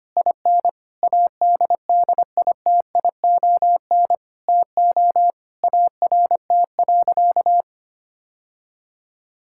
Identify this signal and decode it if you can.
Morse code, heard as a keyed tone, about 25 wpm, 700 Hz. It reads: IN ADDITION TO ART.
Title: Sentences from Top 1000 Words - Review Part 6 - 25wpm